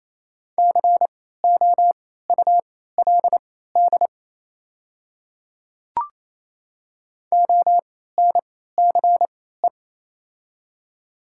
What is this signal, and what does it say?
Morse code, heard as a keyed tone, about 28 wpm, spaced as for 16 wpm, 700 Hz. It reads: COULD ONCE
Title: 3x Character Spacing - Top 300 Words - 28wpm